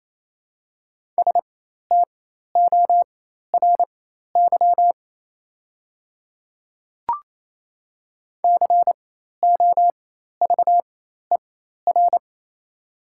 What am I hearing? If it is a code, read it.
Morse code, heard as a keyed tone, about 28 wpm, spaced as for 13 wpm, 690 Hz. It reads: STORY COVER
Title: 4x Character Spacing - Top 300 Words - 28wpm